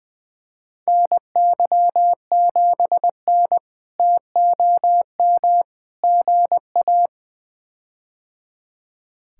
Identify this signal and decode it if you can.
Morse code, heard as a keyed tone, about 21 wpm, 700 Hz. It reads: NY7N TOM GA
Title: K1USN SST QSO Practice - 20wpm